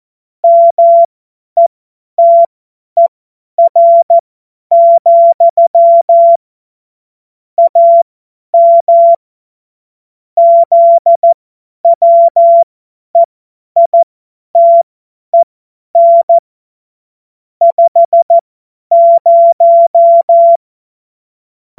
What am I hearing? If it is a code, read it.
Morse code, heard as a keyed tone, about 14 wpm, spaced as for 10 wpm, 680 Hz. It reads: METER, AM ZWEITEN 50
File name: Morsekurs / Witze / Witz 24.Oktober 2023 50 F70